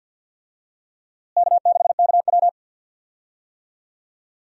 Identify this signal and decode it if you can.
Morse code, heard as a keyed tone, about 45 wpm, 700 Hz. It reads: K6KW
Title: USA Call Signs - 4 Characters - Repeated 2x - 50wpm